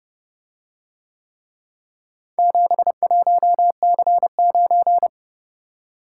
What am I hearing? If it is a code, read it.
Morse code, heard as a keyed tone, about 27 wpm, 700 Hz. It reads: 71C9